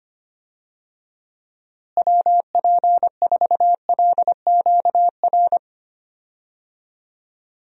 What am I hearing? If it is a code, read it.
Morse code, heard as a keyed tone, about 25 wpm, 700 Hz. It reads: WP4LQR
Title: USA Call Signs - 6 Characters - Repeated 3x - 25wpm